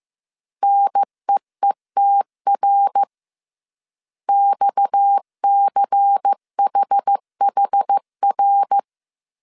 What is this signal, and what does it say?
Morse code, heard as a keyed tone, about 15 wpm, 800 Hz. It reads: NEETR XCHHR
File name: CW / DET / 1 6 1500C S A 14wpm 2921